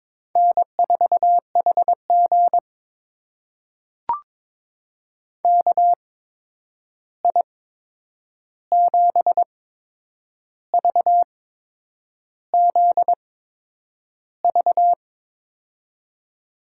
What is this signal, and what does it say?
Morse code, heard as a keyed tone, about 22 wpm, spaced as for 8 wpm, 700 Hz. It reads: N4HG KI7VZV
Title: USA Call Signs - 8x Character Spacing - 22wpm